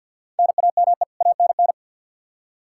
Morse code, heard as a keyed tone, about 49 wpm, 700 Hz. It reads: NAME ANN
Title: K1USN SST QSO Practice 50wpm